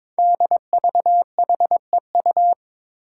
Morse code, heard as a keyed tone, about 22 wpm, 700 Hz. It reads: DVHEU
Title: Five Challenging Letter-Number Combinations 22wpm